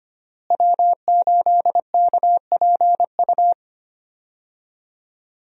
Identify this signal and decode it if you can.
Morse code, heard as a keyed tone, about 25 wpm, 700 Hz. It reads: W8KPU
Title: USA Call Signs - 5 Characters - Repeated 2x - 25wpm